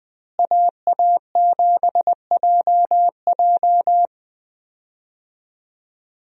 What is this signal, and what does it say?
Morse code, heard as a keyed tone, about 20 wpm, 700 Hz. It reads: AA7JJ
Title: USA Call Signs - 5 Characters - Repeated 3x - 20wpm